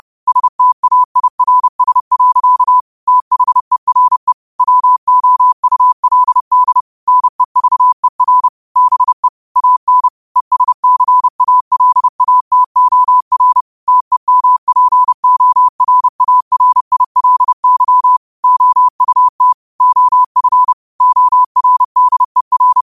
Morse code, heard as a keyed tone, about 30 wpm, 1000 Hz. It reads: STAIRS. THERE WOULD NEVER BE AN ESCALATOR TEMPORARILY OUT OF ORDER
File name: qotdfiles / 30wpm